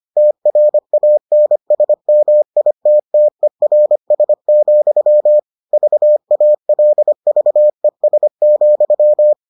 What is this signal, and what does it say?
Morse code, heard as a keyed tone, about 25 wpm, 600 Hz. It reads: TRANSMITTERS, VALVES,